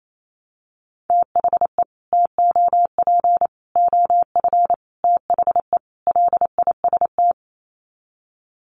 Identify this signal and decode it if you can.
Morse code, heard as a keyed tone, about 28 wpm, 700 Hz. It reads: THE TOP OF THE LIST
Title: Sets of 5 Words Encore2 28wpm